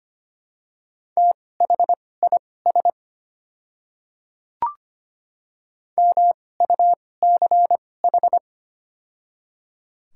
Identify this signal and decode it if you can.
Morse code, heard as a keyed tone, about 25 wpm, spaced as for 18 wpm, 700 Hz. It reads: THIS MUCH
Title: 2x Character Spacing - Top 200 Words - 25wpm